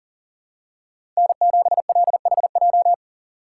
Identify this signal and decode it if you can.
Morse code, heard as a keyed tone, about 40 wpm, 700 Hz. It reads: N7LHJ
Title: USA Call Signs - 5 Characters - 3x Character Spacing - 40wpm